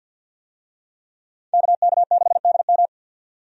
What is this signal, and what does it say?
Morse code, heard as a keyed tone, about 50 wpm, 700 Hz. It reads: KK6DM
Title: USA Call Signs - 5 Characters - 50wpm